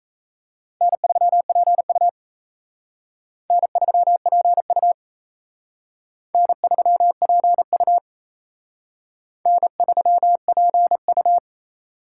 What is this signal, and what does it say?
Morse code, heard as a keyed tone, about 34 wpm, 700 Hz. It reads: N3PU N3PU N3PU N3PU
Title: Speed Racing - USA 1x2 Call Signs 28wpm